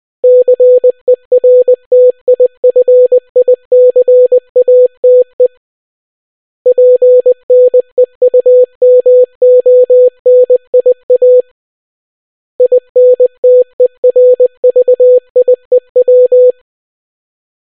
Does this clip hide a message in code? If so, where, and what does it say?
Yes: Morse code from 0.2 to 16.5 s, heard as a keyed tone, about 20 wpm, 500 Hz. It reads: CERTIFICATE PNEUMONIA INTERVIEW